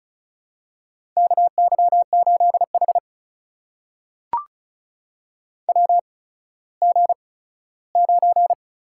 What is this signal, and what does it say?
Morse code, heard as a keyed tone, about 36 wpm, spaced as for 17 wpm, 700 Hz. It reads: KY8H WG9
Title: USA Call Signs - 4 Characters - 8x Character Spacing - 35wpm